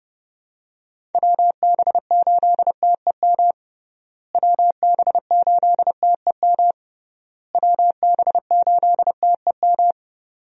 Morse code, heard as a keyed tone, about 30 wpm, 700 Hz. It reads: WB8TEM WB8TEM WB8TEM
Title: USA Call Signs - 6 Characters - Repeated 3x - 30wpm